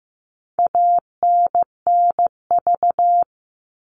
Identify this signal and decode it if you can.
Morse code, heard as a keyed tone, about 15 wpm, 700 Hz. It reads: ANNV
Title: All Cut Numbers - 4-Digit 15wpm